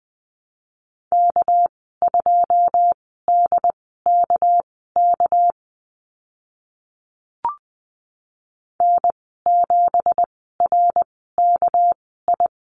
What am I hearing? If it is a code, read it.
Morse code, heard as a keyed tone, about 20 wpm, spaced as for 14 wpm, 700 Hz. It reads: K2DKK N7RKI
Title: 213 Lesson 34g - Number 8 - Call Signs 20wpm